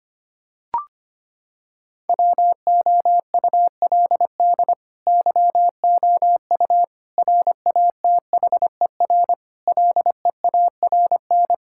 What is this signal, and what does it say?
Morse code, heard as a keyed tone, about 25 wpm, 700 Hz. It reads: WOULD YOU RATHER LEARN
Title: Sentences from Top 1000 Words - Review Part 6 - 25wpm